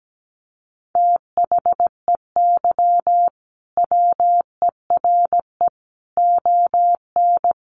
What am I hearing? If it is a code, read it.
Morse code, heard as a keyed tone, about 17 wpm, 700 Hz. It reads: THEY WERE ON
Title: Sentences from Top 1500 Words 17wpm